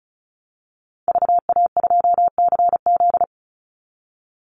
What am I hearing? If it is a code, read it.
Morse code, heard as a keyed tone, about 34 wpm, 700 Hz. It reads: VA2CZ
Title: Call Signs Repeated 3x - 35wpm